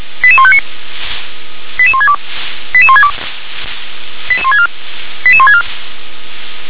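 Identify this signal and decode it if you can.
Five-tone selective call (ZVEI), heard as quick runs of five tones, tones of about 70 ms. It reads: 88159 88162 88162 88164 88164
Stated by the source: ZVEI five-tone selective call received by SDR.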